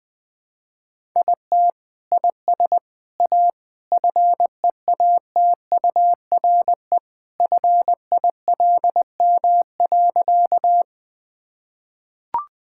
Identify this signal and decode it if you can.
Morse code, heard as a keyed tone, about 20 wpm, 700 Hz. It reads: IT IS A FEATURE FILM.
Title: Sentences from Top 1000 Words - Review Part 5 - 20wpm